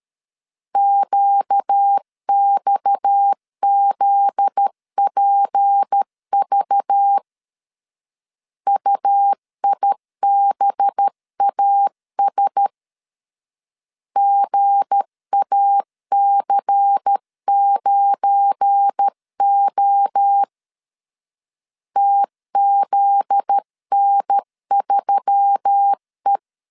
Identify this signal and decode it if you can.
Morse code, heard as a keyed tone, about 13 wpm, 800 Hz. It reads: QXZPV UIBAS GAC9O TZN3E
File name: LEZIONI / 7 GF9Z / 7 GF9Z 12wpm 2147